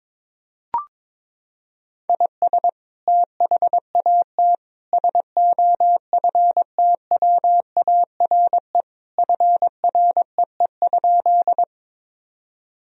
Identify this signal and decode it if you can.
Morse code, heard as a keyed tone, about 22 wpm, 700 Hz. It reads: IS THAT SOFTWARE FREE?
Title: Sentences from Top 1000 Words - Review Part 3 - 22wpm